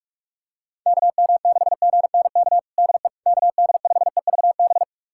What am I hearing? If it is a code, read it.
Morse code, heard as a keyed tone, about 42 wpm, 690 Hz. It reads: KM6GNK DE KD5EVB
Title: USA Callsign DE Callsign - 6 Characters - 45wpm